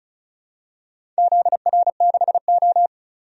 Morse code, heard as a keyed tone, about 36 wpm, 700 Hz. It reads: ZR6O
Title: DXCC-100 Call Signs - 4 Characters - 35wpm